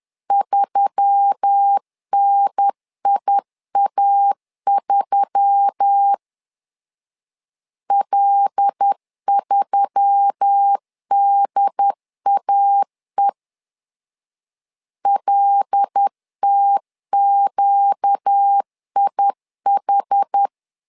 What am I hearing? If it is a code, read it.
Morse code, heard as a keyed tone, about 10 wpm, 800 Hz. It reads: 3NIA3 L3DAE LTQIH
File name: LEZIONI / 5 LBVC / 5 L 10wpm 2259